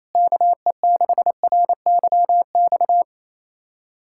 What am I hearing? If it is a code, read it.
Morse code, heard as a keyed tone, about 28 wpm, 700 Hz. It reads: KE6RYX
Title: USA Callsign DE Callsign - 6 Characters - 28wpm